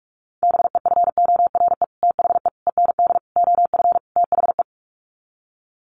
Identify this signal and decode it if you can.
Morse code, heard as a keyed tone, about 44 wpm, 700 Hz. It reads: BEFORE THE END OF THE
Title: Sets of 5 Words Encore 45wpm